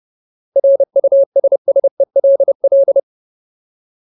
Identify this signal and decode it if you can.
Morse code, heard as a keyed tone, about 30 wpm, 550 Hz. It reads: RUSSELL
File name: Ultimate Word List / 30WPM Ultimate